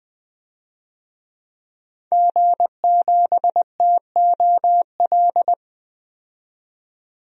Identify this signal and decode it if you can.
Morse code, heard as a keyed tone, about 20 wpm, 700 Hz. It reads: G7TOL
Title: DXCC-100 Call Signs - 5 Characters - 20wpm